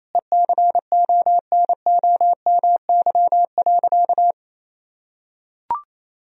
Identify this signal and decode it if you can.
Morse code, heard as a keyed tone, about 27 wpm, 700 Hz. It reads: ECONOMY.
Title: Sentences from Top 1000 Words - Review Part 3 - 28wpm